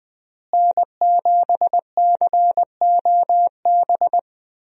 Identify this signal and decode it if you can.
Morse code, heard as a keyed tone, about 20 wpm, 700 Hz. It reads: N7COB
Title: USA Call Signs - 5 Characters - Repeated 3x - 20wpm